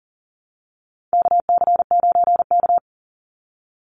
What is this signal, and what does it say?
Morse code, heard as a keyed tone, about 40 wpm, 700 Hz. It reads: KC9K
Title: Call Signs - 40wpm